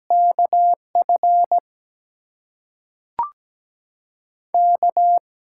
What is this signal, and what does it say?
Morse code, heard as a keyed tone, about 17 wpm, spaced as for 5 wpm, 700 Hz. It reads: KFK
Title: USA Call Signs - 6x Character Spacing - 17wpm